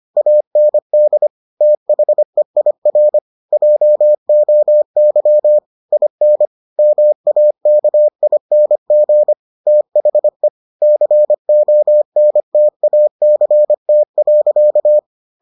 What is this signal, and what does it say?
Morse code, heard as a keyed tone, about 25 wpm, 600 Hz. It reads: AND THEIR JOY IN MAKING THE CONTACT.